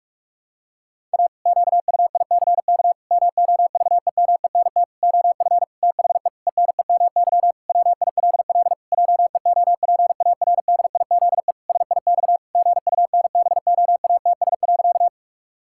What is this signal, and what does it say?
Morse code, heard as a keyed tone, about 41 wpm, 700 Hz. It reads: A QUICK MOVEMENT OF THE ENEMY WILL JEOPARDIZE SIX GUNBOATS.
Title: Pangram Challenge v2 - 45wpm